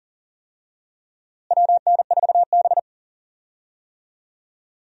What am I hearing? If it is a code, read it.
Morse code, heard as a keyed tone, about 40 wpm, 700 Hz. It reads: WN4B
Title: USA Call Signs - 4 Characters - Repeated 3x - 40wpm